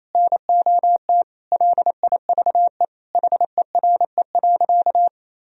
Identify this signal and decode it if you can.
Morse code, heard as a keyed tone, about 27 wpm, 700 Hz. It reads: NOT LIVE HERE.
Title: Sentences from Top 1000 Words - Review Part 1 - 28wpm